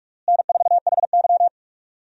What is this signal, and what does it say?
Morse code, heard as a keyed tone, about 42 wpm, 700 Hz. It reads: N4HY
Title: USA Call Signs - 4 Characters - Repeated 2x - 45wpm